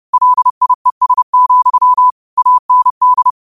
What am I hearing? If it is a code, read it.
Morse code, heard as a keyed tone, about 30 wpm, 1000 Hz. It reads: LIES, AND
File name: qotdfiles / 30wpm